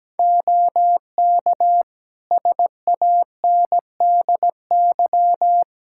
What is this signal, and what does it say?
Morse code, heard as a keyed tone, about 17 wpm, 700 Hz. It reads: OK SANDY
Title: Practice QSO - Exchange 3 - More Info - 17wpm